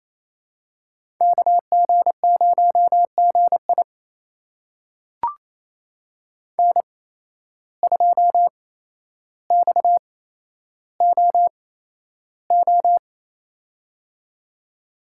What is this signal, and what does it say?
Morse code, heard as a keyed tone, about 28 wpm, spaced as for 12 wpm, 700 Hz. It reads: KG0GI N2XOO
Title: USA Call Signs - 5 Characters - 8x Character Spacing - 28wpm